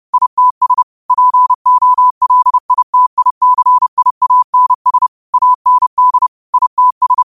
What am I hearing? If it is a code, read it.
Morse code, heard as a keyed tone, about 30 wpm, 1000 Hz. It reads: ITS POLITICIANS AND ITS
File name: qotdfiles / 30wpm